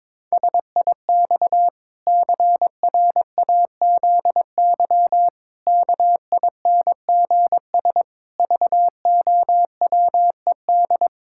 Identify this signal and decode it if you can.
Morse code, heard as a keyed tone, about 22 wpm, 700 Hz. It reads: SIX CRAZY KINGS VOWED
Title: Pangram Challenge v2 - 22wpm